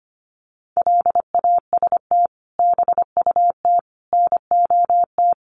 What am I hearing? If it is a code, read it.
Morse code, heard as a keyed tone, about 25 wpm, 700 Hz. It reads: LAST BUT NOT
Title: Sentences from Top 700 Words 25wpm